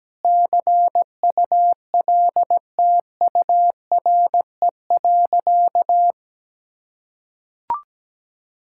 Morse code, heard as a keyed tone, about 17 wpm, 700 Hz. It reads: CULTURE.
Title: Sentences from Top 1000 Words - Review Part 6 - 17wpm